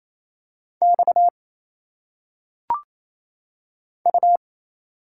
Morse code, heard as a keyed tone, about 28 wpm, 700 Hz. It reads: X U
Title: Single Letters - Phonetic Pronunciation 28wpm